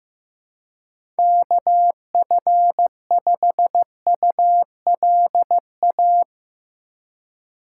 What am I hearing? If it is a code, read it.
Morse code, heard as a keyed tone, about 15 wpm, 700 Hz. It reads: KF5ULA
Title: USA Call Signs - 6 Characters - 3x Character Spacing - 15wpm